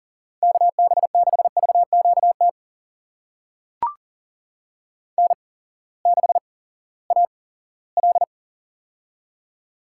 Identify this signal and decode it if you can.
Morse code, heard as a keyed tone, about 39 wpm, spaced as for 20 wpm, 700 Hz. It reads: KB6VQT N6AL
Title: USA Call Signs - 8x Character Spacing - 40wpm